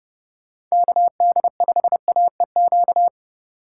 Morse code, heard as a keyed tone, about 32 wpm, 700 Hz. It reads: KD5AEQ